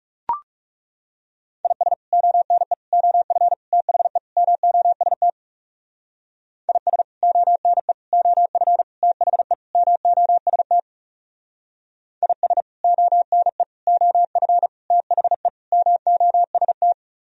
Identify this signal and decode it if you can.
Morse code, heard as a keyed tone, about 39 wpm, 700 Hz. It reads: IS ONE OF THE MOST IS ONE OF THE MOST IS ONE OF THE MOST